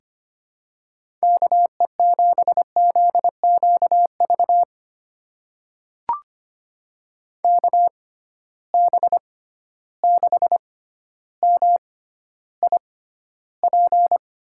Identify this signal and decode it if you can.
Morse code, heard as a keyed tone, about 25 wpm, spaced as for 13 wpm, 700 Hz. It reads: KE7ZQV KB6MIP
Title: USA Call Signs - 6 Characters - 6x Character Spacing - 25wpm